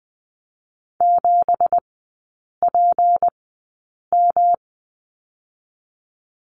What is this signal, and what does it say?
Morse code, heard as a keyed tone, about 20 wpm, spaced as for 8 wpm, 700 Hz. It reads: 7PM
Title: 166 Lesson 27e - Number 7 - Three Characters 20wpm